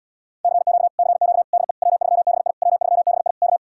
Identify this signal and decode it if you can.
Morse code, heard as a keyed tone, about 80 wpm, 700 Hz. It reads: CQ CQ DE W3BI W3BI K